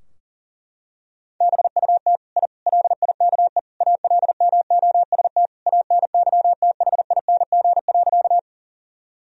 Morse code, heard as a keyed tone, about 40 wpm, 700 Hz. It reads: BUT I LIKE ALMOST ANYTHING.
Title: Sentences from Top 1000 Words - Review Part 4 - 40wpm